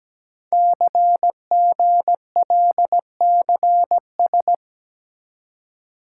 Morse code, heard as a keyed tone, about 17 wpm, 700 Hz. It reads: CGLCS
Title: Code Groups - 5 Letters - 17wpm